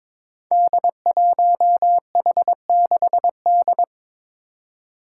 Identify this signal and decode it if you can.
Morse code, heard as a keyed tone, about 22 wpm, 700 Hz. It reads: D1H6D
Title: Five Challenging Letter-Number Combinations 22wpm